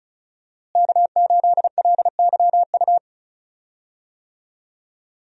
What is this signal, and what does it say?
Morse code, heard as a keyed tone, about 36 wpm, 700 Hz. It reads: K8LYU
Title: USA Call Signs - 5 Characters - Repeated 3x - 35wpm